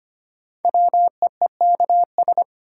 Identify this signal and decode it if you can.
Morse code, heard as a keyed tone, about 25 wpm, 700 Hz. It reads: WEEKS
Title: Sets of 5 Words - New Word Drill - 25wpm